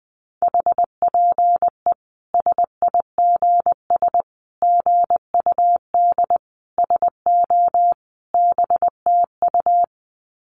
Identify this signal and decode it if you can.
Morse code, heard as a keyed tone, about 20 wpm, 700 Hz. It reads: HPE SIGS GUD SO BTU
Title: Common QSO Phrases 20wpm